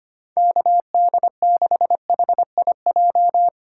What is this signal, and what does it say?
Morse code, heard as a keyed tone, about 25 wpm, 700 Hz. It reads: KD6HIJ
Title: USA Call Signs - 6 Characters - 8x Character Spacing - 25wpm